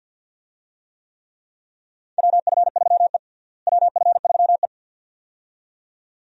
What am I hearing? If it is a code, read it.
Morse code, heard as a keyed tone, about 51 wpm, 700 Hz. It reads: WF3E WF3E
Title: USA Call Signs - 4 Characters - Repeated 2x - 50wpm